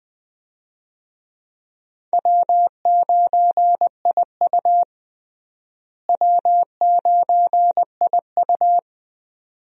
Morse code, heard as a keyed tone, about 20 wpm, 700 Hz. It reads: W9IU W9IU
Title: USA Call Signs - 4 Characters - Repeated 2x - 20wpm